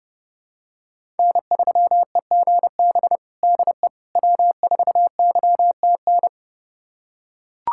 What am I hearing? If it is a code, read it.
Morse code, heard as a keyed tone, about 30 wpm, 700 Hz. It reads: N3EGB DE W4YTN
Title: USA Callsign DE Callsign - 5 Characters - 30wpm